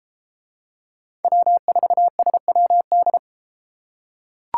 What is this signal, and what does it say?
Morse code, heard as a keyed tone, about 32 wpm, 700 Hz. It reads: W4SWD